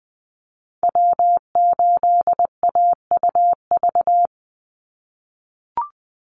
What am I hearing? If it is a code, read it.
Morse code, heard as a keyed tone, about 20 wpm, 700 Hz. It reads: W8AUV
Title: USA Call Signs 20wpm